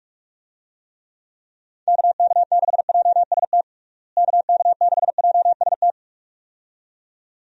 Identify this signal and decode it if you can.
Morse code, heard as a keyed tone, about 46 wpm, 700 Hz. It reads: KK6JST KK6JST
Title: USA Call Signs - 6 Characters - Repeated 2x - 45wpm